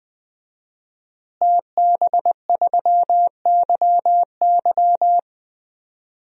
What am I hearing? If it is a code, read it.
Morse code, heard as a keyed tone, about 21 wpm, 700 Hz. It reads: TB3YY
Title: DXCC-100 Call Signs - 5 Characters - Repeated 2x - 20wpm